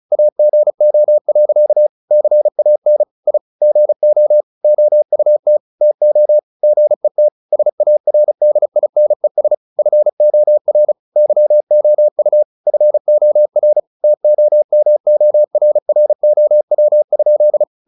Morse code, heard as a keyed tone, about 34 wpm, 600 Hz. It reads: AGO. CAN I GO OUT TO GET SARDINES FOR YOU FOR TOMORROW?
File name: the old man and the sea / the old man and the sea 1 35wpm